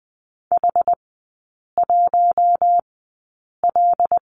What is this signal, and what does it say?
Morse code, heard as a keyed tone, about 20 wpm, spaced as for 8 wpm, 700 Hz. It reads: H1L
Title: 074 Lesson 13e - Letter L - Three Characters 20wpm